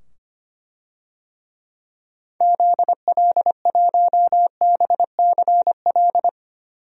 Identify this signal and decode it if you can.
Morse code, heard as a keyed tone, about 25 wpm, 700 Hz. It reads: ZL1BCL
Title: DXCC-100 Call Signs - 6 Characters - 25wpm